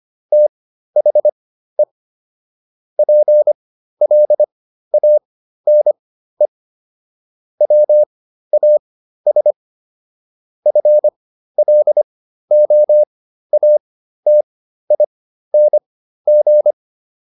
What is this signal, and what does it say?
Morse code, heard as a keyed tone, about 25 wpm, spaced as for 13 wpm, 600 Hz. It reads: THE PLANE WAS FLOATING